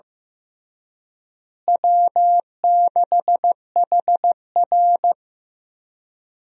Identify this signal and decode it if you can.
Morse code, heard as a keyed tone, about 15 wpm, 700 Hz. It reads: W6HR
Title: USA Call Signs - 4 Characters - Repeated 3x - 15wpm